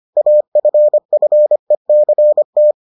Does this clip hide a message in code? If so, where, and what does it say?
Yes: Morse code from 0.2 to 2.7 s, heard as a keyed tone, about 25 wpm, 600 Hz. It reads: AFFECT